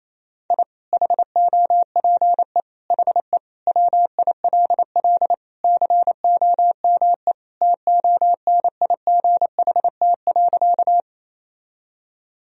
Morse code, heard as a keyed tone, about 28 wpm, 700 Hz. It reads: I HOPE HE WILL COME TONIGHT.
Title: Sentences from Top 1000 Words - Review Part 3 - 28wpm